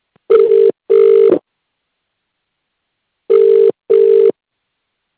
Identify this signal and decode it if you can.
Ringback tone (UK), 400 + 450 Hz.